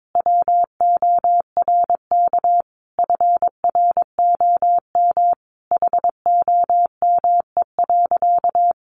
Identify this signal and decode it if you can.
Morse code, heard as a keyed tone, about 22 wpm, 700 Hz. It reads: WORK FROM HOME.
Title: Sentences from Top 1500 Words 22wpm